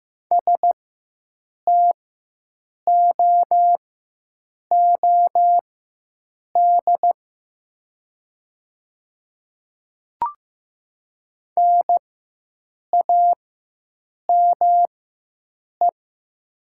Morse code, heard as a keyed tone, about 15 wpm, spaced as for 7 wpm, 700 Hz. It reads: STOOD NAME
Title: 4x Character Spacing - Top 500 Words - 15wpm